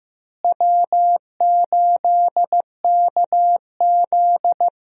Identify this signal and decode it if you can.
Morse code, heard as a keyed tone, about 15 wpm, 700 Hz. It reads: W8KZ
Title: USA Call Signs - 4 Characters - Repeated 2x - 15wpm